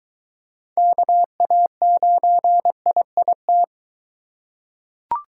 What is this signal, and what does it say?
Morse code, heard as a keyed tone, about 22 wpm, 700 Hz. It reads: KA9IIT